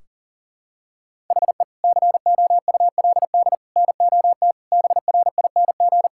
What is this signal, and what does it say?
Morse code, heard as a keyed tone, about 37 wpm, 700 Hz. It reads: HE COULD NOT BRING
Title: Sentences from Top 1600 Words 40wpm